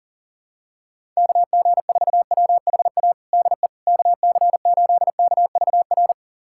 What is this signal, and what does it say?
Morse code, heard as a keyed tone, about 40 wpm, 700 Hz. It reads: KG4WHA DE KC8KVR
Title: USA Callsign DE Callsign - 6 Characters - 40wpm